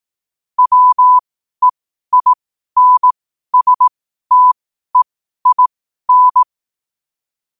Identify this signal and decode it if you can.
Morse code, heard as a keyed tone, about 18 wpm, spaced as for 12 wpm, 1000 Hz. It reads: WEINSTEIN
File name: qotdfiles / 12wpm